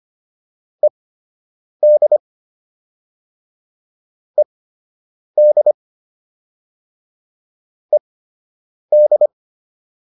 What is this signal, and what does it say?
Morse code, heard as a keyed tone, about 25 wpm, spaced as for 8 wpm, 600 Hz. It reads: ED ED ED